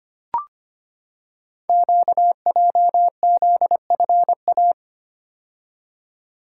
Morse code, heard as a keyed tone, about 25 wpm, 700 Hz. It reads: QJZFA
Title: Code Groups - 5 Letters - 25wpm